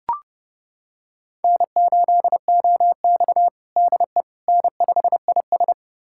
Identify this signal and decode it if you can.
Morse code, heard as a keyed tone, about 30 wpm, 700 Hz. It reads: N8OX DE N5IS
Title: USA Callsign DE Callsign - 4 Characters - 30wpm